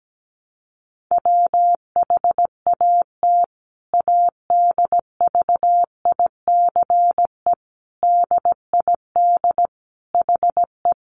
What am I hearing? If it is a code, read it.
Morse code, heard as a keyed tone, about 17 wpm, 700 Hz. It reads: WHAT ADVICE DID HE
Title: Sentences from Top 1100 Words 17wpm